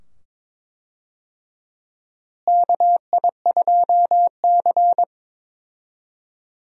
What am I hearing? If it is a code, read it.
Morse code, heard as a keyed tone, about 21 wpm, 700 Hz. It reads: KI2C
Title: USA Call Signs - 4 Characters - Repeated 2x - 22wpm